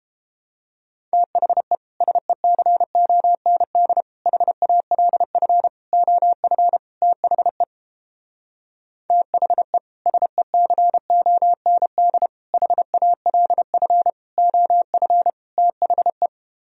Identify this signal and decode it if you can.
Morse code, heard as a keyed tone, about 31 wpm, 700 Hz. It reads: THE SECOND HALF OF THE THE SECOND HALF OF THE